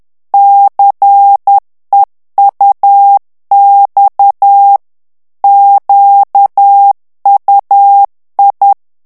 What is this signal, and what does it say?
Morse code, heard as a keyed tone, about 10 wpm, 800 Hz. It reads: CEUX QUI
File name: FRANCAIS TEXTE CLAIR 2 VITESSE 10 MOTS MINUTES